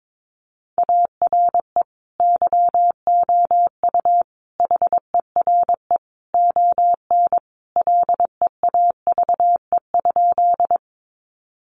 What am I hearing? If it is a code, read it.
Morse code, heard as a keyed tone, about 22 wpm, 700 Hz. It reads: ARE YOU HERE ON LEAVE?
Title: Sentences from Top 400 Words 22wpm